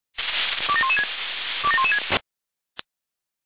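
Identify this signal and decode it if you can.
Five-tone selective call (ZVEI), heard as quick runs of five tones, tones of about 70 ms. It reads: 27116 27116
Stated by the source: ZVEI five-tone selective call received on SDR, address 27036.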